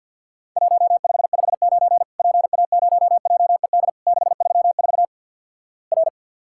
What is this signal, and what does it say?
Morse code, heard as a keyed tone, about 50 wpm, 700 Hz. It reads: 1559 PA0JED 634 R
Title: Worked All Europe DX Contest - QTC Exchange with Repeats - 50wpm